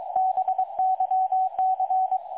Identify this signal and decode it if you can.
Morse code, heard as a keyed tone, about 23 wpm, 750 Hz. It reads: BYC
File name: dlazaawansowanych / nagraniazpasma / websdr recording 3565.0kHz